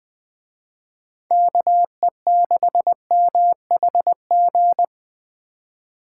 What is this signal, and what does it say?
Morse code, heard as a keyed tone, about 20 wpm, 700 Hz. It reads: KE6MHG
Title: Speed Racing - USA 2x3 Call Signs 17wpm